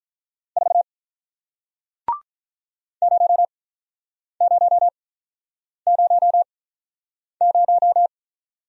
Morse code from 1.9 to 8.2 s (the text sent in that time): 0000